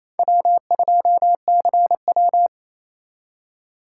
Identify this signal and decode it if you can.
Morse code, heard as a keyed tone, about 36 wpm, 700 Hz. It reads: W2CW